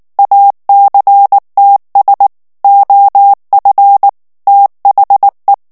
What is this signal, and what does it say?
Morse code, heard as a keyed tone, about 19 wpm, 800 Hz. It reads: ACTS OF THE
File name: ANGLAIS texte clair 3 vitesse 18 mots minute